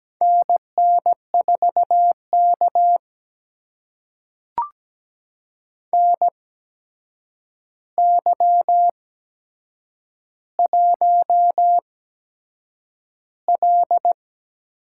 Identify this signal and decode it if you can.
Morse code, heard as a keyed tone, about 17 wpm, spaced as for 7 wpm, 700 Hz. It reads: NN4K NY1L
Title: USA Call Signs - 4 Characters - 8x Character Spacing - 17wpm